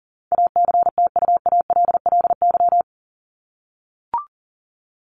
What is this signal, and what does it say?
Morse code, heard as a keyed tone, about 40 wpm, 700 Hz. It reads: ACTUALLY
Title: Top 501-750 Words 40wpm